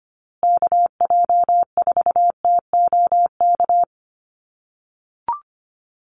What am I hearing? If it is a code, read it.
Morse code, heard as a keyed tone, about 25 wpm, 700 Hz. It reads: KJ4TOK
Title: USA Call Signs Repeated 3x 25wpm